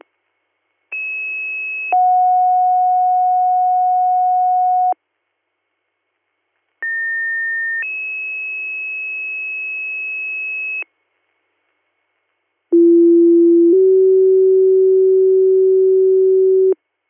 Two-tone page, heard as a one-second tone, then a three-second tone, about 2600 then 730 Hz, about 1800 then 2600 Hz, about 340 then 380 Hz. It is heard from 0.9 to 16.7 s.